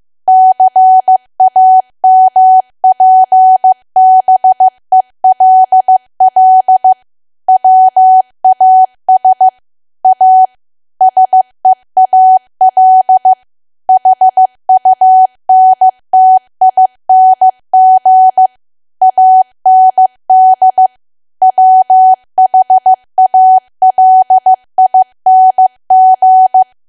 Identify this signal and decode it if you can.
Morse code, heard as a keyed tone, about 15 wpm, 750 Hz. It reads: CAMPBELL WAS A SEAL HUNTING AND WHALING
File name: cwa / W1AW-Level2 / 15WPM